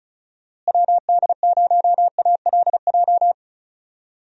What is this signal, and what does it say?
Morse code, heard as a keyed tone, about 34 wpm, 700 Hz. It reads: WD0ALJ
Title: USA Call Signs - 6 Characters - Repeated 3x - 35wpm